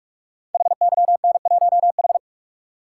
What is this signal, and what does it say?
Morse code, heard as a keyed tone, about 44 wpm, 700 Hz. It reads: HYN1H
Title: Code Groups - 5 Letter:Numbers - 45wpm